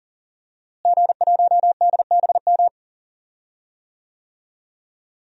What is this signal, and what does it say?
Morse code, heard as a keyed tone, about 38 wpm, 700 Hz. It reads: G1DBM
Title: DXCC-100 Call Signs - 5 Characters - Repeated 3x - 40wpm